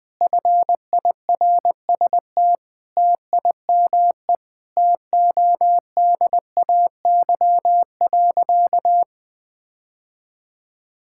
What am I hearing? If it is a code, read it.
Morse code, heard as a keyed tone, about 20 wpm, 700 Hz. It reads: FIRST TIME TODAY.
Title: Sentences with 4-Word Phrases - 20wpm